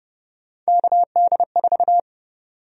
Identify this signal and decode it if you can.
Morse code, heard as a keyed tone, about 32 wpm, 700 Hz. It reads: KD4
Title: USA Call Sign Prefixes - 30wpm